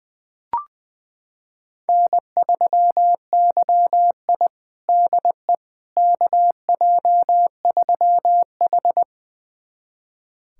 Morse code, heard as a keyed tone, about 20 wpm, 700 Hz. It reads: N3YI DE KJ3H
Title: USA Callsign DE Callsign - 4 Characters - 20wpm